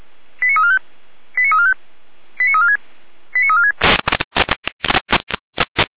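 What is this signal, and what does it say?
Five-tone selective call (ZVEI), heard as quick runs of five tones, tones of about 80 ms. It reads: 89345 89345 89346 89346
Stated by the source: ZVEI five-tone selective call received on SDR.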